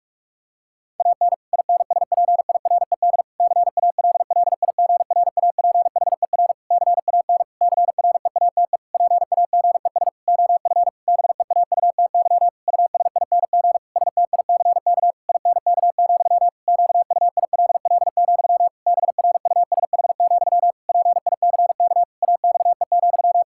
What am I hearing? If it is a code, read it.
Morse code, heard as a keyed tone, about 46 wpm, 700 Hz. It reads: AN INSPIRED CALLIGRAPHER CAN CREATE PAGES OF BEAUTY USING STICK INK, QUILL, BRUSH, PICK AXE,
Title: Pangram Challenge v2 - 45wpm